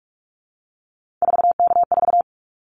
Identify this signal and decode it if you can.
Morse code, heard as a keyed tone, about 44 wpm, 700 Hz. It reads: 4K4
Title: Callsign Prefixes for Top 100 DXCC - 45wpm